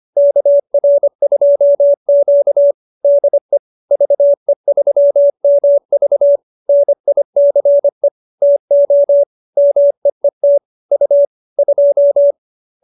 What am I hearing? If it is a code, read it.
Morse code, heard as a keyed tone, about 25 wpm, 590 Hz. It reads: KR2Q DE VE3MV NICE TO MEET U 2